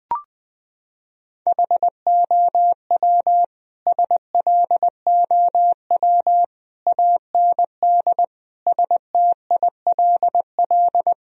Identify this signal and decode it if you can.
Morse code, heard as a keyed tone, about 20 wpm, 700 Hz. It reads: HOW SLOW AND STILL
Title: Sentences from Top 1000 Words - Review Part 2 - 20wpm